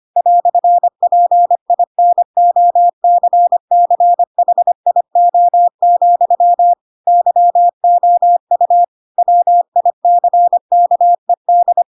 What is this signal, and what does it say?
Morse code, heard as a keyed tone, about 25 wpm, 700 Hz. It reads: "PINOCCHIO, YOU WICKED
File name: Pinocchio / Pinocchio ch3 25 25 ik1hge